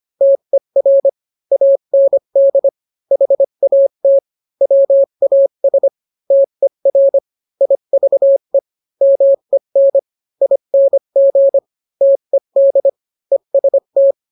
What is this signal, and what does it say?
Morse code, heard as a keyed tone, about 25 wpm, 550 Hz. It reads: TER AND HAT WAS TER IVE MEN ING TED EST